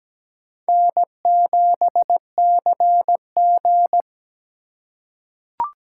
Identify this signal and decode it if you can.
Morse code, heard as a keyed tone, about 17 wpm, 700 Hz. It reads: N7CG
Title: USA Call Signs - 4 Characters - 3x Character Spacing - 17wpm